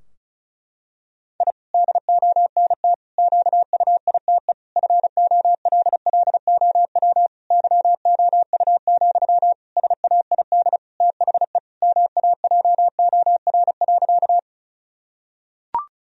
Morse code, heard as a keyed tone, about 36 wpm, 700 Hz. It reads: I DONT QUITE FOLLOW YOU, SAID THE MAJOR.
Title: Sentences from Top 1000 Words - Review Part 4 - 35wpm